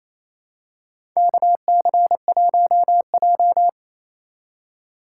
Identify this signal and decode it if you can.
Morse code, heard as a keyed tone, about 28 wpm, 700 Hz. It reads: KC1J
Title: USA Call Signs - 4 Characters - 2x Character Spacing - 28wpm